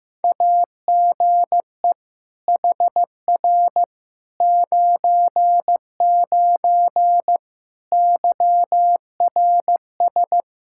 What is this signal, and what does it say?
Morse code, heard as a keyed tone, about 15 wpm, 700 Hz. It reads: AGE HR 99 YRS
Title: Practice QSO - Exchange 3 - More Info - 15wpm